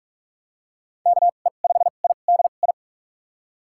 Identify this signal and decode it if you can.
Morse code, heard as a keyed tone, about 42 wpm, spaced as for 32 wpm, 700 Hz. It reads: KE5IDI
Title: USA Call Signs - 6 Characters - 2x Character Spacing - 45wpm